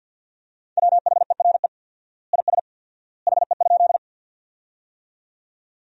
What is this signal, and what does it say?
Morse code, heard as a keyed tone, about 50 wpm, 700 Hz. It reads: WHERE IS HE?
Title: 4x Word-Spacing - Sentences from Top 200 Words - 50wpm